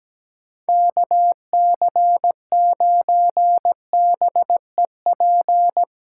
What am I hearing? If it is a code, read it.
Morse code, heard as a keyed tone, about 19 wpm, 700 Hz. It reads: KC9BEP
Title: Speed Racing - USA Call Signs 17wpm